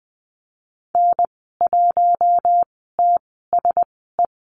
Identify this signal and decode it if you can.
Morse code, heard as a keyed tone, about 20 wpm, spaced as for 15 wpm, 700 Hz. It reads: N1TSE
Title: 082 Lesson 14g - Number 2 - Call Signs 20wpm